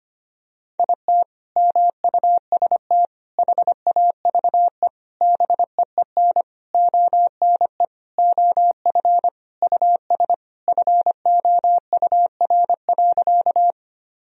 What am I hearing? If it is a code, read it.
Morse code, heard as a keyed tone, about 25 wpm, 700 Hz. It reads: IT MUST HAVE BEEN ONE OF US FOUR.
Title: Sentences from Top 1000 Words - Review Part 1 - 25wpm